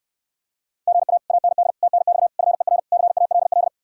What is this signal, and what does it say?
Morse code, heard as a keyed tone, about 69 wpm, 700 Hz. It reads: GA AND TNX FER CALL